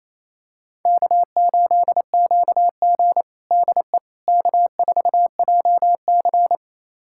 Morse code, heard as a keyed tone, about 28 wpm, 700 Hz. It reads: K8QG DE K4JC
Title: USA Callsign DE Callsign - 4 Characters - 28wpm